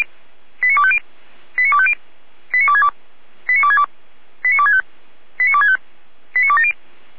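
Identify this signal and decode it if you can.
Five-tone selective call (ZVEI), heard as quick runs of five tones, tones of about 70 ms. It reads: 89250 89250 89262 89262 89265 89265 89270
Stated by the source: Five-tone ZVEI selective call 89303 received by SDR.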